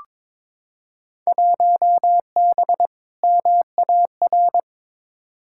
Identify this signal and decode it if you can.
Morse code, heard as a keyed tone, about 22 wpm, 700 Hz. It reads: 1B MAR
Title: Field Day Exchanges Encore - 22wpm